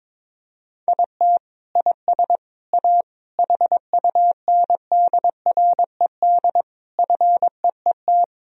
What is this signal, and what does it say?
Morse code, heard as a keyed tone, about 22 wpm, 700 Hz. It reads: IT IS A HUNDRED FEET
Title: Sentences from Top 1000 Words - Review Part 2 - 22wpm